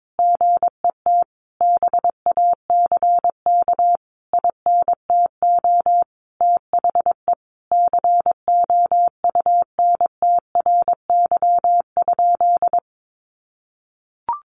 Morse code, heard as a keyed tone, about 22 wpm, 700 Hz. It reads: GET BACK INTO THE COUNTRY?
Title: Sentences from Top 1400 Words 22wpm